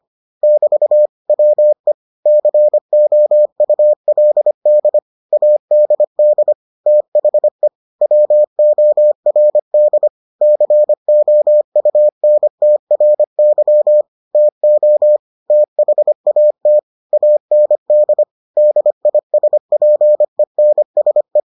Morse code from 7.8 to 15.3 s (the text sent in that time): WORD COUNTRY TO